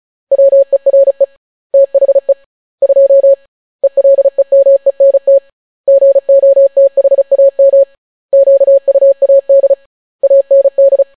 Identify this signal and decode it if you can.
Morse code, heard as a keyed tone, about 36 wpm, 550 Hz. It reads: WERE THE 2 ELEMENT GOTHAM QUAD AND